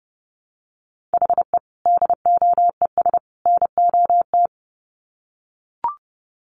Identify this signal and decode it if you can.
Morse code, heard as a keyed tone, about 30 wpm, 700 Hz. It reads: HE DOES NOT
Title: Sets of 3 Words 30wpm